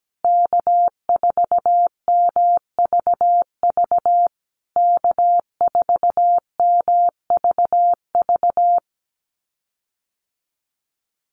Morse code, heard as a keyed tone, about 17 wpm, 700 Hz. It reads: K4MVV K4MVV
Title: USA Call Signs Repeated 3x 17wpm